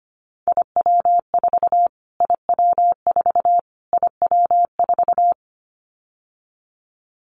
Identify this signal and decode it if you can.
Morse code, heard as a keyed tone, about 25 wpm, 700 Hz. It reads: IW4 IW4 IW4
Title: Callsign Prefixes for Top 25 DXCC - Repeated 3x - 25wpm